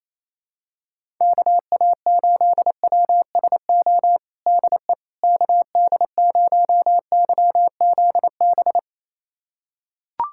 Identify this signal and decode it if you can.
Morse code, heard as a keyed tone, about 28 wpm, 700 Hz. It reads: KA8WSO DE KD0YZB
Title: USA Callsign DE Callsign - 6 Characters - 28wpm